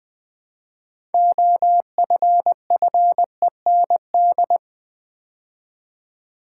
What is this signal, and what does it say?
Morse code, heard as a keyed tone, about 20 wpm, 690 Hz. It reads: OFFEND
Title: Build A Word - 2 Letters At A Time - 6-Letter Words - 20wpm